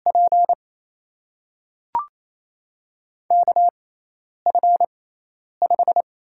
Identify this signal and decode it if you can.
Morse code, heard as a keyed tone, about 28 wpm, spaced as for 10 wpm, 700 Hz. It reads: P KF5
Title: USA Call Signs - 5 Characters - 6x Character Spacing - 28wpm